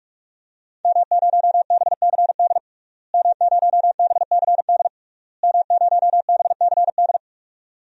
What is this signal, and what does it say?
Morse code, heard as a keyed tone, about 46 wpm, 700 Hz. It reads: M0BCD M0BCD M0BCD
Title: DXCC-100 Call Signs - 5 Characters - Repeated 3x - 45wpm